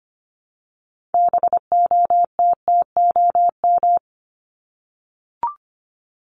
Morse code, heard as a keyed tone, about 25 wpm, 700 Hz. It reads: BOTTOM
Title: Top 751-1000 Words 25wpm